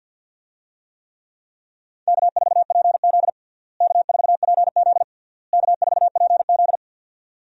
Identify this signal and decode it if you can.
Morse code, heard as a keyed tone, about 46 wpm, 700 Hz. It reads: K4PZ K4PZ K4PZ
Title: USA Call Signs - 4 Characters - Repeated 3x - 50wpm